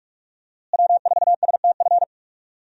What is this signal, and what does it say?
Morse code, heard as a keyed tone, about 46 wpm, 700 Hz. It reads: W4STF
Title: USA Call Signs - 5 Characters - Repeated 2x - 45wpm